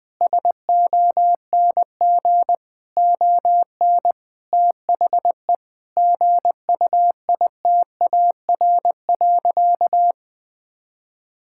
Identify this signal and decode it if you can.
Morse code, heard as a keyed tone, about 20 wpm, 700 Hz. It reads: SONG ON THE GUITAR.
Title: Difficult Sentences with Hints - 20wpm